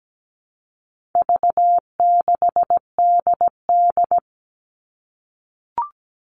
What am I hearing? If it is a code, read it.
Morse code, heard as a keyed tone, about 17 wpm, 700 Hz. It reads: V6DD
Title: All Cut Numbers - 4-Digit 17wpm